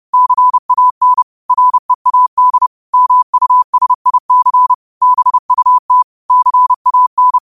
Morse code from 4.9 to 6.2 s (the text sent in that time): BUT